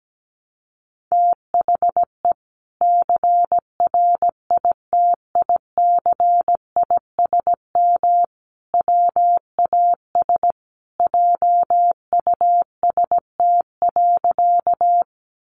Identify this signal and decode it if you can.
Morse code, heard as a keyed tone, about 17 wpm, 700 Hz. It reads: THE CRITICISM WAS JUST.
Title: Sentences from Top 1300 Words 17wpm